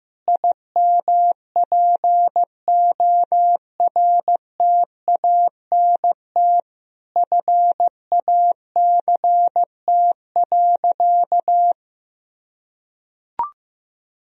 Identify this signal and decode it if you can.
Morse code, heard as a keyed tone, about 15 wpm, 700 Hz. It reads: IMPORTANT FACT.
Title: Sentences from Top 1000 Words - Review Part 3 - 15wpm